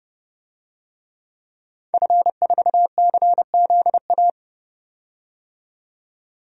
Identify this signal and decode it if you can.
Morse code, heard as a keyed tone, about 31 wpm, 700 Hz. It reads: F4CZA
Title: DXCC-100 Call Signs - 5 Characters - 30wpm